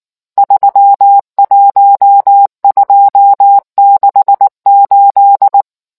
Morse code, heard as a keyed tone, about 19 wpm, 800 Hz. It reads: 31268